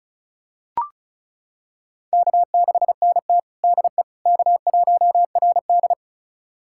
Morse code, heard as a keyed tone, about 34 wpm, 700 Hz. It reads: K6NT DE K1RD
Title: USA Callsign DE Callsign - 4 Characters - 35wpm